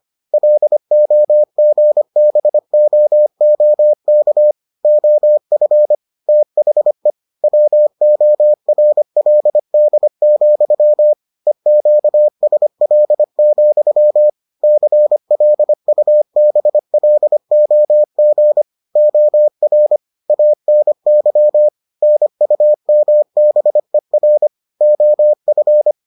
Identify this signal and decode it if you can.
Morse code, heard as a keyed tone, about 25 wpm, 600 Hz. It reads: LOGBOOK OF THE WORLD, EQSL, CLUBLOG OR ANY NUMBER OF